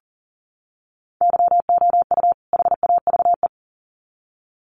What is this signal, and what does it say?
Morse code, heard as a keyed tone, about 40 wpm, 700 Hz. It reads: YOU HAVE
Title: Sets of 2 Words 40wpm